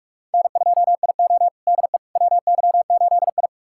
Morse code, heard as a keyed tone, about 46 wpm, 700 Hz. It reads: N2IO DE WY8I
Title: USA Callsign DE Callsign - 4 Characters - 45wpm